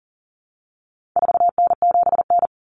Morse code, heard as a keyed tone, about 40 wpm, 700 Hz. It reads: 4N7N
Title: Common Cut Numbers - 4-Digit 40wpm